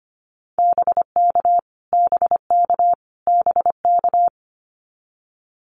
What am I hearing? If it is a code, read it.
Morse code, heard as a keyed tone, about 25 wpm, 700 Hz. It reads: BK BK BK
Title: Abbreviations Repeated 3x - 25wpm